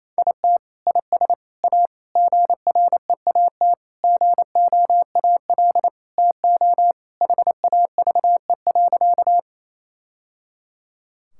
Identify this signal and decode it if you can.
Morse code, heard as a keyed tone, about 28 wpm, 700 Hz. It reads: IT IS A GREAT GOAL TO HAVE.
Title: Sentences from Top 1000 Words - Review Part 5 - 28wpm